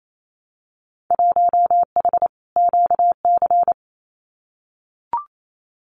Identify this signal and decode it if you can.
Morse code, heard as a keyed tone, about 28 wpm, 700 Hz. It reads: 1H QC
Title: Winter Field Day Exchanges 28wpm